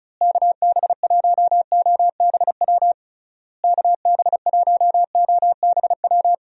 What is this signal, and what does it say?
Morse code, heard as a keyed tone, about 34 wpm, 700 Hz. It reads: KB1OBW KB1OBW
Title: USA Call Signs - 6 Characters - Repeated 3x - 35wpm